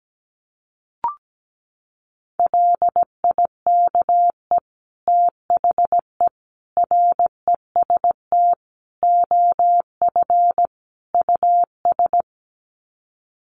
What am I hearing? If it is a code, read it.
Morse code, heard as a keyed tone, about 17 wpm, 700 Hz. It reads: LIKE THE REST OF US
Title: Sets of 5 Words Encore2 17wpm